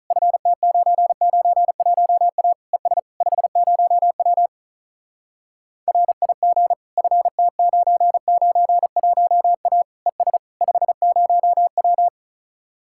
Morse code, heard as a keyed tone, about 44 wpm, 700 Hz. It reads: FT991A ES 50W RIG FT991A ES 50W